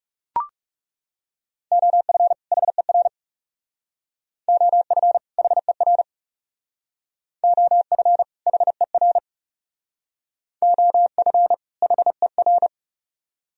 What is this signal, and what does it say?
Morse code, heard as a keyed tone, about 38 wpm, 700 Hz. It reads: OF HER OF HER OF HER OF HER
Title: Speed-Racing - Sets of 2 Words 30wpm